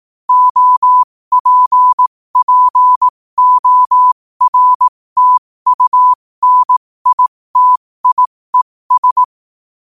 Morse code, heard as a keyed tone, about 18 wpm, spaced as for 15 wpm, 1000 Hz. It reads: OPPORTUNITIES
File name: qotdfiles / 15wpm